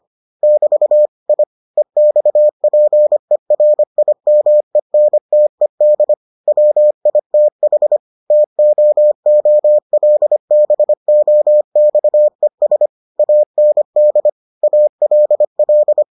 Morse code, heard as a keyed tone, about 25 wpm, 600 Hz. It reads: <BT> I EXPERIMENTED WITH TOOLBOXES AND ALL